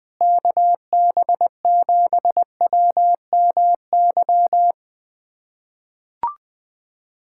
Morse code, heard as a keyed tone, about 20 wpm, 700 Hz. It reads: KB7WMY
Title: USA Callsign DE Callsign - 6 Characters - 20wpm